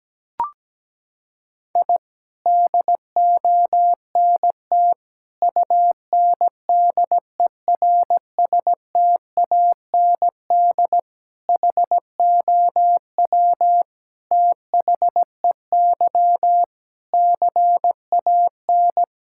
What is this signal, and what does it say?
Morse code, heard as a keyed tone, about 17 wpm, 700 Hz. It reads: I DONT UNDERSTAND HOW THEY CAN
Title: Sentences from Top 1000 Words - Review Part 5 - 17wpm